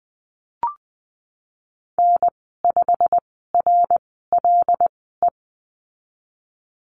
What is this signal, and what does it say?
Morse code, heard as a keyed tone, about 19 wpm, spaced as for 14 wpm, 700 Hz. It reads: N5RLE
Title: 088 Lesson 15g - Number 5 - Call Signs 20wpm